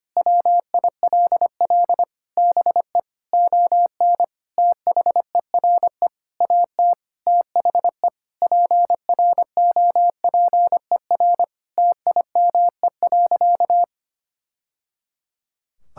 Morse code, heard as a keyed tone, about 25 wpm, 700 Hz. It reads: WILL BE ON THERE AT THE PROPER TIME.
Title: Sentences from Top 1000 Words - Review Part 5 - 25wpm